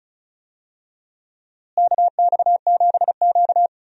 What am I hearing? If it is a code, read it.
Morse code, heard as a keyed tone, about 36 wpm, 700 Hz. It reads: KX7Q
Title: USA Call Signs - 4 Characters - Repeated 3x - 35wpm